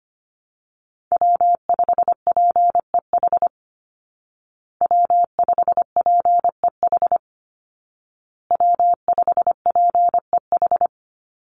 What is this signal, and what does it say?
Morse code, heard as a keyed tone, about 25 wpm, 700 Hz. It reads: W5PEH W5PEH W5PEH
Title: Call Signs Repeated 2x - 25wpm